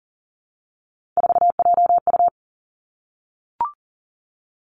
Morse code, heard as a keyed tone, about 42 wpm, 700 Hz. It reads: 4JU
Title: Three Challenging Letter-Number Combinations 40wpm